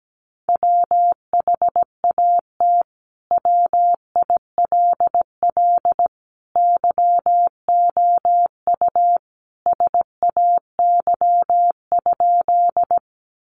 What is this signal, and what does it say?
Morse code, heard as a keyed tone, about 17 wpm, 700 Hz. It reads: WHAT WILL YOU SAY?
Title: Sentences from Top 200 Words 17wpm